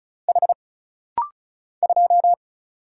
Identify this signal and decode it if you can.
Morse code, heard as a keyed tone, about 36 wpm, 700 Hz. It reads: H 2
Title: Single Letter-Number - Rapid-Fire 35wpm